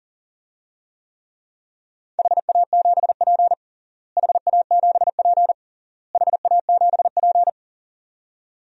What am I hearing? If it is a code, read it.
Morse code, heard as a keyed tone, about 40 wpm, 700 Hz. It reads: HA7P HA7P HA7P
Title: DXCC-100 Call Signs - 4 Characters - Repeated 3x - 40wpm